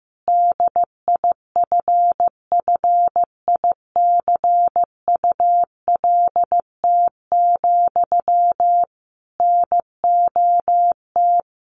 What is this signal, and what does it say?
Morse code, heard as a keyed tone, about 14 wpm, 700 Hz. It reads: DIFFICULT, NOT
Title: Sentences from Top 1200 Words 15wpm